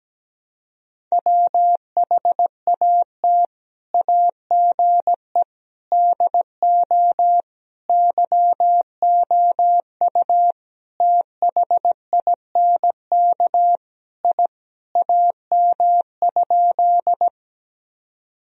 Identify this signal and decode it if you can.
Morse code, heard as a keyed tone, about 17 wpm, 700 Hz. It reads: WHAT AGE DO YOU THINK I AM?
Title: Sentences from Top 1000 Words - Review Part 3 - 17wpm